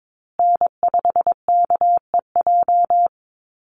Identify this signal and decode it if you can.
Morse code, heard as a keyed tone, about 22 wpm, 700 Hz. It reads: N5KEJ
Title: USA Call Signs Repeated 2x 22wpm